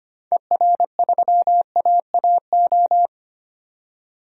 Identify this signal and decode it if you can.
Morse code, heard as a keyed tone, about 25 wpm, 700 Hz. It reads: ER3AAO
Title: DXCC-100 Call Signs - 6 Characters - 25wpm